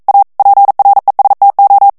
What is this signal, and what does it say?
Morse code, heard as a keyed tone, about 42 wpm, 800 Hz. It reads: A PRESTO
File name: morse mp3 / ITALIEN texte clair 1 vitesse 40 mots minute